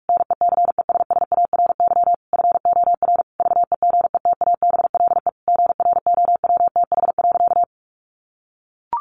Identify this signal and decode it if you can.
Morse code, heard as a keyed tone, about 44 wpm, 700 Hz. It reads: NECESSARY FOR VEGETABLE GROWTH.
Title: Sentences from Top 1300 Words 45wpm